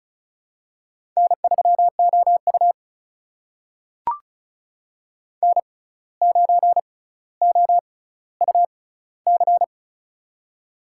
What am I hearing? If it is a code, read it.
Morse code, heard as a keyed tone, about 36 wpm, spaced as for 17 wpm, 700 Hz. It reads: N3OU N9OUC
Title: USA Call Signs - 6x Character Spacing - 35wpm